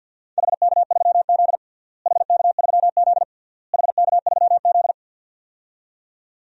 Morse code, heard as a keyed tone, about 53 wpm, 700 Hz. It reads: HK3Z HK3Z HK3Z
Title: DXCC-100 Call Signs - 4 Characters - Repeated 3x - 50wpm